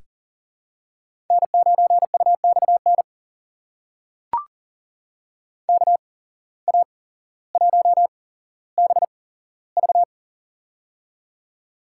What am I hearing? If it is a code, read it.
Morse code, heard as a keyed tone, about 40 wpm, spaced as for 17 wpm, 700 Hz. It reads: N9UXN KA1BV
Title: USA Call Signs - 8x Character Spacing - 40wpm